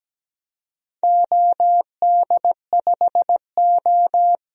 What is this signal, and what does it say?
Morse code, heard as a keyed tone, about 17 wpm, 700 Hz. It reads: OD5O
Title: DXCC-100 Call Signs - 4 Characters - 17wpm